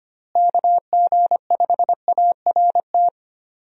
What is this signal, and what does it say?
Morse code, heard as a keyed tone, about 25 wpm, 700 Hz. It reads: KG5ART
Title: USA Call Signs - 6 Characters - Repeated 2x - 25wpm